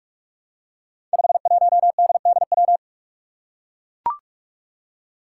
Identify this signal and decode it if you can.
Morse code, heard as a keyed tone, about 44 wpm, 700 Hz. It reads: 51DDW
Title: Code Groups - 5 Letter:Numbers - 45wpm